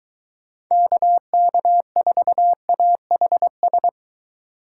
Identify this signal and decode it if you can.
Morse code, heard as a keyed tone, about 23 wpm, 700 Hz. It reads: KK4AHS